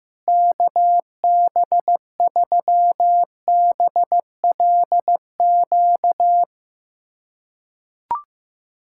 Morse code, heard as a keyed tone, about 15 wpm, 700 Hz. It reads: KB3BLQ
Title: USA Callsign DE Callsign - 6 Characters - 15wpm